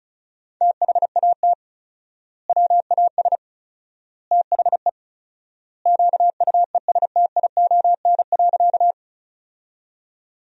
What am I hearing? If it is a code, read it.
Morse code, heard as a keyed tone, about 36 wpm, 700 Hz. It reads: THAT WAS THE QUESTION.
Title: 4x Word-Spacing - Sentences from Top 400 Words - 35wpm